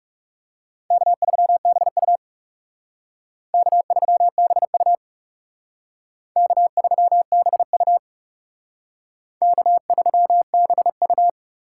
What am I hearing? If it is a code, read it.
Morse code, heard as a keyed tone, about 36 wpm, 700 Hz. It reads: K3BU K3BU K3BU K3BU
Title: Speed Racing - USA Call Signs 30wpm